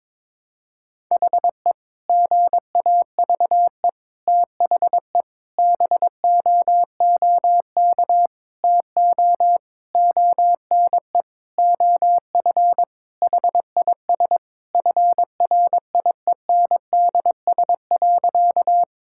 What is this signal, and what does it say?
Morse code, heard as a keyed tone, about 22 wpm, 700 Hz. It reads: HE GAVE THE BOOK TO ONE OF HIS FRIENDS.
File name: practice / 22wpm / Sentences with 4-Word Phrases - 22wpm